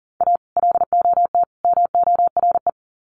Morse code, heard as a keyed tone, about 40 wpm, 700 Hz. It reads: A LOT MORE
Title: Sets of 3 Words 40wpm